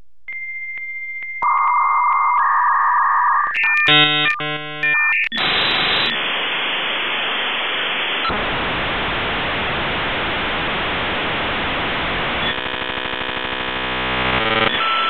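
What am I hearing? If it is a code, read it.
Dial-up modem handshake.
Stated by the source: Dial-up modem connection sound.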